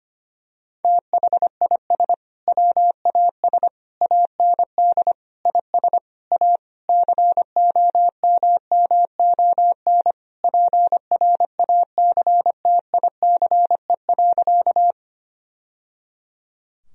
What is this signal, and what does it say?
Morse code, heard as a keyed tone, about 25 wpm, 700 Hz. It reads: THIS WAS AND IS A COMMON PRACTICE.
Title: Sentences from Top 1000 Words - Review Part 4 - 25wpm